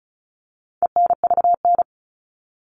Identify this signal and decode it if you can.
Morse code, heard as a keyed tone, about 34 wpm, 700 Hz. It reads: ENVN
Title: All Cut Numbers - 4-Digit 35wpm